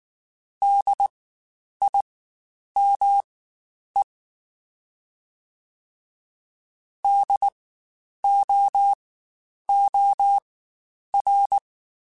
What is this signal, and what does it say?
Morse code, heard as a keyed tone, about 19 wpm, spaced as for 9 wpm, 800 Hz. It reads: DIME DOOR
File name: Spaced Alphas / Spaced Alphas mp3s / Ds sp @18